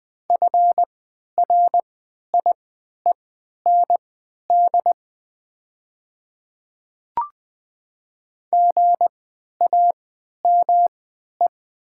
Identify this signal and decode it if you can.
Morse code, heard as a keyed tone, about 20 wpm, spaced as for 11 wpm, 700 Hz. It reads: FRIEND GAME
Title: 3x Character Spacing - Top 500 Words - 20wpm